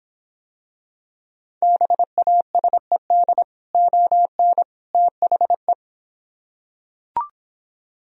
Morse code, heard as a keyed tone, about 26 wpm, 700 Hz. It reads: BASED ON THE